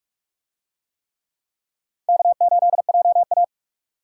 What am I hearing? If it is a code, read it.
Morse code, heard as a keyed tone, about 43 wpm, 700 Hz. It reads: K8JA